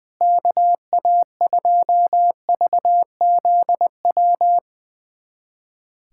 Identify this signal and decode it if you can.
Morse code, heard as a keyed tone, about 20 wpm, 700 Hz. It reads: KA2VZW
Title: USA Callsign DE Callsign 20wpm